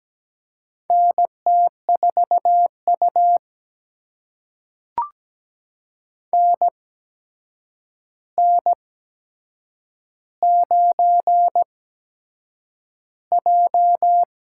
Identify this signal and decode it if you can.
Morse code, heard as a keyed tone, about 17 wpm, spaced as for 7 wpm, 700 Hz. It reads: NT4U NN9J
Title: USA Call Signs - 4 Characters - 8x Character Spacing - 17wpm